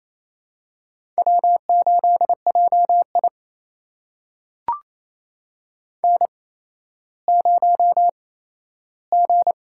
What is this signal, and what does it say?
Morse code, heard as a keyed tone, about 28 wpm, spaced as for 14 wpm, 700 Hz. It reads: W8JI N0G
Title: USA Call Signs - 8x Character Spacing - 28wpm